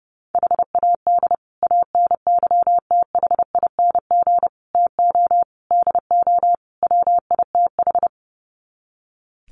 Morse code, heard as a keyed tone, about 30 wpm, 700 Hz. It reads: HAD ANYTHING TO DO WITH
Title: Sets of 5 Words Encore 30wpm